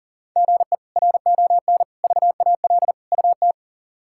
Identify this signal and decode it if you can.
Morse code, heard as a keyed tone, about 40 wpm, 700 Hz. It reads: GE RON VAL UT
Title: K1USN SST QSO Practice - 40wpm